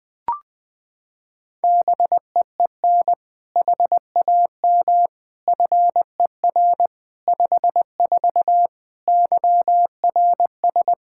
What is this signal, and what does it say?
Morse code, heard as a keyed tone, about 21 wpm, 700 Hz. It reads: BEEN HAM FER 54 YRS
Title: K1USN SST QSO Practice 20wpm